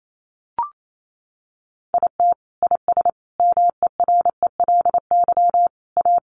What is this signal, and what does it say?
Morse code, heard as a keyed tone, about 28 wpm, 700 Hz. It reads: IT IS MERELY A
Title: Sentences from Top 1400 Words 28wpm